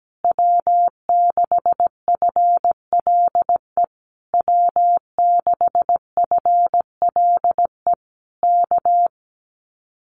Morse code, heard as a keyed tone, about 17 wpm, 700 Hz. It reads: W6FLE W6FLE K
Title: Common QSO Phrases Encore 17wpm